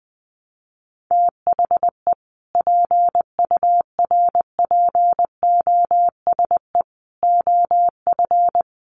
Morse code, heard as a keyed tone, about 20 wpm, 700 Hz. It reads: THE PURPOSE OF
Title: Sentences from Top 800 Words 20wpm